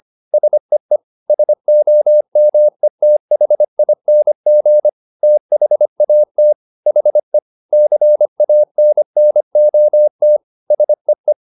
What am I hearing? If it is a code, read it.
Morse code, heard as a keyed tone, about 25 wpm, 600 Hz. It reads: SEE SOMETHING THAT HE CANNOT SEE
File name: the old man and the sea part 1 25wpm